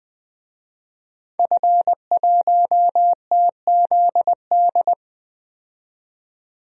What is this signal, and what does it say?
Morse code, heard as a keyed tone, about 20 wpm, 700 Hz. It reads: F1TZD
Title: DXCC-100 Call Signs - 5 Characters - 20wpm